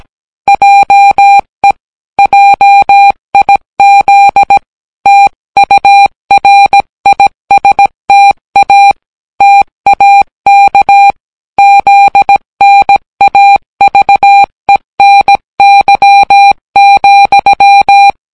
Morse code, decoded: JE JIZ TURISTA TAK ZNAVENY,